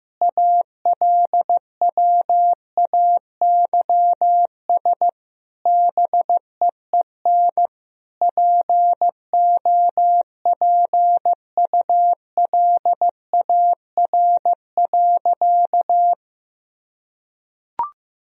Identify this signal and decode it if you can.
Morse code, heard as a keyed tone, about 15 wpm, 700 Hz. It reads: ALWAYS BEEN POPULAR.
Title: Sentences from Top 1000 Words - Review Part 3 - 15wpm